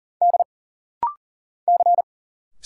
Morse code, heard as a keyed tone, about 40 wpm, 700 Hz. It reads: D C
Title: Single Letter-Number Alt - Rapid-Fire 40wpm